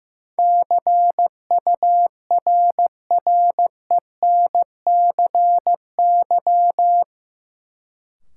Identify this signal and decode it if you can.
Morse code, heard as a keyed tone, about 15 wpm, 700 Hz. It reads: CURRENCY
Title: Sentences from Top 1500 Words - New Word Drill - 15wpm